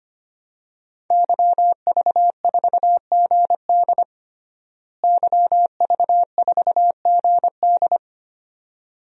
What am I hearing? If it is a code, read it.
Morse code, heard as a keyed tone, about 25 wpm, 700 Hz. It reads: YV4GD YV4GD
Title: DXCC-100 Call Signs - 5 Characters - Repeated 2x - 25wpm